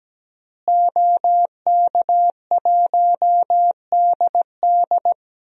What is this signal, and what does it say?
Morse code, heard as a keyed tone, about 17 wpm, 700 Hz. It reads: OK1DD
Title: DXCC-100 Call Signs - 5 Characters - Repeated 2x - 17wpm